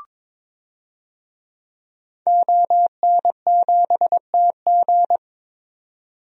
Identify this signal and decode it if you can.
Morse code, heard as a keyed tone, about 22 wpm, 700 Hz. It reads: ON7TG
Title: DXCC-100 Call Signs - 5 Characters - 22wpm